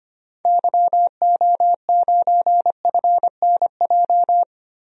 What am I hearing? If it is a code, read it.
Morse code, heard as a keyed tone, about 25 wpm, 700 Hz. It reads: YO9FNJ
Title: DXCC-100 Call Signs - 6 Characters - 25wpm